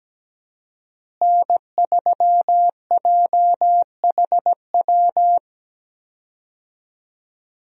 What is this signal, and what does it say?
Morse code, heard as a keyed tone, about 17 wpm, 700 Hz. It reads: N3JHW
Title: USA Call Signs - 5 Characters - Repeated 2x - 17wpm